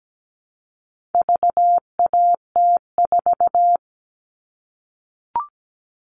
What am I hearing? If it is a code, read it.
Morse code, heard as a keyed tone, about 17 wpm, 700 Hz. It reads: VAT4
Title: All Cut Numbers - 4-Digit 17wpm